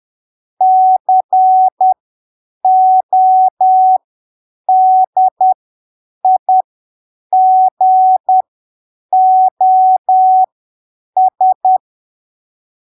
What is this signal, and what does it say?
Morse code, heard as a keyed tone, about 10 wpm, spaced as for 7 wpm, 750 Hz. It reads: CODIGOS
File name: arquivos / texto / texto 04 10ppm